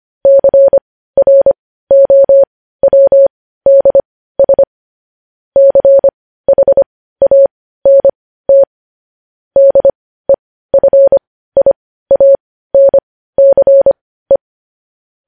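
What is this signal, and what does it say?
Morse code, heard as a keyed tone, about 25 wpm, spaced as for 15 wpm, 550 Hz. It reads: CROWDS CHANT DEFIANCE
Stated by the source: Morse code practice of news headlines at 15 wpm, opening with vvv.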